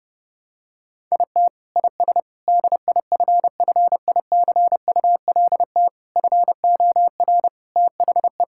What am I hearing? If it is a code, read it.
Morse code, heard as a keyed tone, about 32 wpm, 700 Hz. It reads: IT IS DIFFICULT FOR THE
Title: Sentences from Top 1000 Words - Review Part 6 - 30wpm